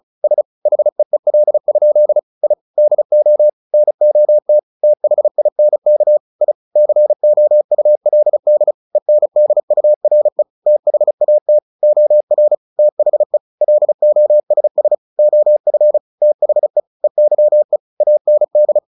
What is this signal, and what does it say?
Morse code, heard as a keyed tone, about 34 wpm, 600 Hz. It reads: S HEEL? I DO NOT THINK I COULD ENDURE THAT OR THE LOSS OF THE EYE AND
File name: the old man and the sea / the old man and the sea 1 35wpm